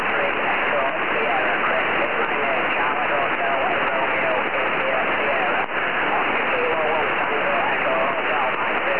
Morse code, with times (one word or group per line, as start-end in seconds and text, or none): none